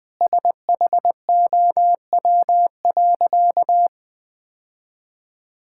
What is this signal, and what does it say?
Morse code, heard as a keyed tone, about 21 wpm, 700 Hz. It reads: SHOW.
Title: Sentences with 4-Word Phrases - Spoken Beginning - 20wpm